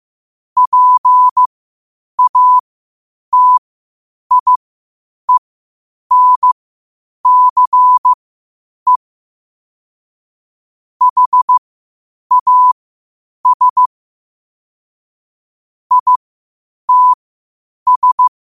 Morse code, heard as a keyed tone, about 15 wpm, spaced as for 8 wpm, 1000 Hz. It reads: PATIENCE HAS ITS
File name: qotdfiles / 8wpm